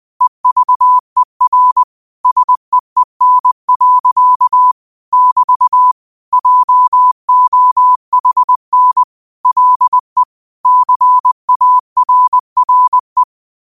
Morse code, heard as a keyed tone, about 20 wpm, 1000 Hz. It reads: EVER SEEN. <BT> JOHN LE CARRE
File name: qotdfiles / 20wpm